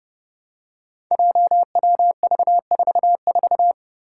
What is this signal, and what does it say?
Morse code, heard as a keyed tone, about 31 wpm, 700 Hz. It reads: JWV44
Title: Five Challenging Letter-Number Combinations 30wpm